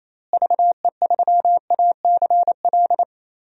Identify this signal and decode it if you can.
Morse code, heard as a keyed tone, about 28 wpm, 700 Hz. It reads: VE3ACL
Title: DXCC-100 Call Signs - 6 Characters - Repeated 3x - 28wpm